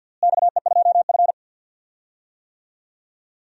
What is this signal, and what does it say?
Morse code, heard as a keyed tone, about 52 wpm, 700 Hz. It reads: XE2F
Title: DXCC-100 Call Signs - 4 Characters - Repeated 3x - 50wpm